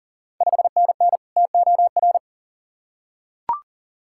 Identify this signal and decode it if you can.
Morse code, heard as a keyed tone, about 40 wpm, 700 Hz. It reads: 5NN TOR
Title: Ontario QSO Party Exchange 40wpm